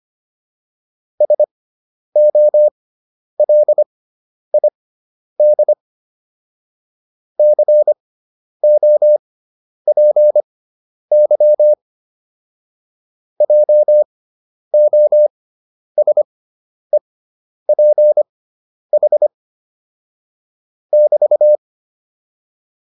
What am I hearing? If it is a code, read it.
Morse code, heard as a keyed tone, about 25 wpm, spaced as for 10 wpm, 600 Hz. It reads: SOLID COPY JOSEPH <BT>
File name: QSO302 10